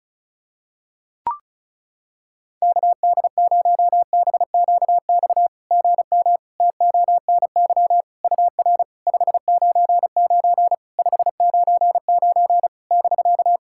Morse code, heard as a keyed tone, about 37 wpm, 700 Hz. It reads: KD0BQX GM TONY UR 599 599 <BK>
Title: SOTA QSO Practice 35wpm